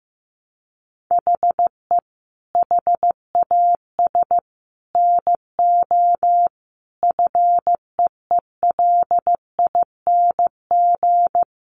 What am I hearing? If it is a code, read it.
Morse code, heard as a keyed tone, about 15 wpm, 700 Hz. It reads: HE HAS NO FEELING
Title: Sentences from Top 1200 Words 15wpm